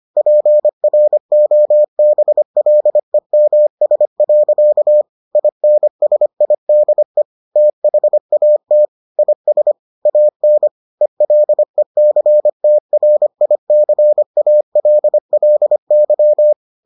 Morse code, heard as a keyed tone, about 25 wpm, 600 Hz. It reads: PROBLEMS. INSIDE THAT IS AN ELECTRICALLY